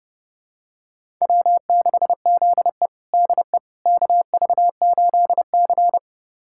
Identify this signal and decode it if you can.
Morse code, heard as a keyed tone, about 30 wpm, 700 Hz. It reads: W6ZE DE KV8C
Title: USA Callsign DE Callsign - 4 Characters - 30wpm